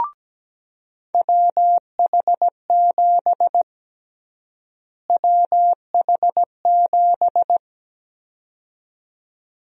Morse code, heard as a keyed tone, about 17 wpm, 700 Hz. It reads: WH7 WH7
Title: USA Call Sign Prefixes - Repeated 2x - 17wpm